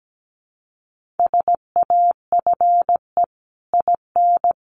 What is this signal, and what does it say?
Morse code, heard as a keyed tone, about 17 wpm, 700 Hz. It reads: SAFE IN
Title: Sentences from Top 900 Words 17wpm